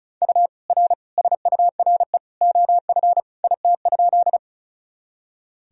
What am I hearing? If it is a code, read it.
Morse code, heard as a keyed tone, about 36 wpm, 700 Hz. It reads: U R SURE OF IT?
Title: Sentences from Top 300 Words and Abbreviations 35wpm